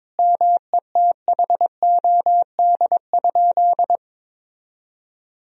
Morse code, heard as a keyed tone, about 22 wpm, 700 Hz. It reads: METHOD?
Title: Sentences from Top 1000 Words - Review Part 3 - 22wpm